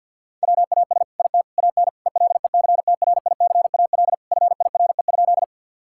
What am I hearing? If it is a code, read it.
Morse code, heard as a keyed tone, about 50 wpm, 700 Hz. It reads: WAS IT AN ELECTRICAL FIRE?
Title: Sentences from Top 1000 Words - Review Part 6 - 50wpm